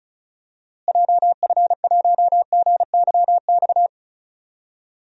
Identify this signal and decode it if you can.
Morse code, heard as a keyed tone, about 36 wpm, 700 Hz. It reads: JF1GYX
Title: DXCC-100 Call Signs - 6 Characters - Repeated 3x - 35wpm